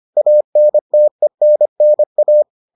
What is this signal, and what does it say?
Morse code, heard as a keyed tone, about 25 wpm, 600 Hz. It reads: ANTENNA